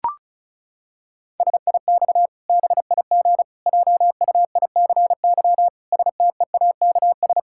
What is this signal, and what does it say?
Morse code, heard as a keyed tone, about 36 wpm, 700 Hz. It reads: SIX BIG JUICY STEAKS
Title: Pangram Challenge v2 - 35wpm